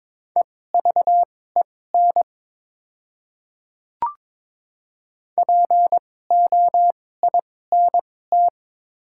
Morse code, heard as a keyed tone, about 22 wpm, spaced as for 16 wpm, 700 Hz. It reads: EVEN POINT
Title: 2x Character Spacing - Top 300 Words - 22wpm